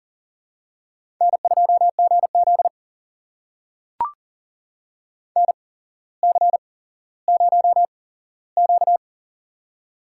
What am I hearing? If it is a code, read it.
Morse code, heard as a keyed tone, about 40 wpm, spaced as for 17 wpm, 700 Hz. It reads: N2GZ NC0Q
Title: USA Call Signs - 4 Characters - 8x Character Spacing - 40wpm